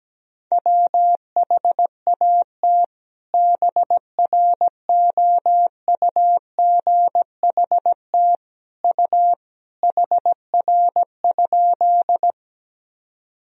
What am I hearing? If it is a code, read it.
Morse code, heard as a keyed tone, about 17 wpm, 700 Hz. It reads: WHAT BROUGHT U HR?
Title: Sentences from Top 500 Words and Abbreviations 17wpm